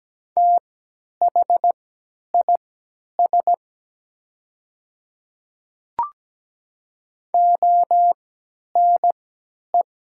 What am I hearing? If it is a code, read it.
Morse code, heard as a keyed tone, about 17 wpm, spaced as for 10 wpm, 700 Hz. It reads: THIS ONE
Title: 3x Character Spacing - Top 100 Words - 17wpm